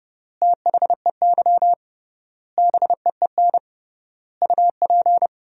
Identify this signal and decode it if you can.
Morse code, heard as a keyed tone, about 30 wpm, 700 Hz. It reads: THEY BEEN UP
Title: 3x Word-Spacing - Sentences from Top 100 Words - 30wpm